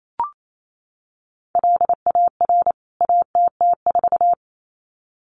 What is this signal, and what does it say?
Morse code, heard as a keyed tone, about 28 wpm, 700 Hz. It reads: LAR ATT4
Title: CWT Exchange 28wpm